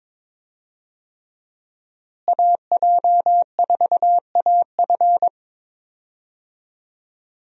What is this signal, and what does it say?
Morse code, heard as a keyed tone, about 22 wpm, 700 Hz. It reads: AJ4AF
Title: USA Call Signs - 5 Characters - Repeated 3x - 22wpm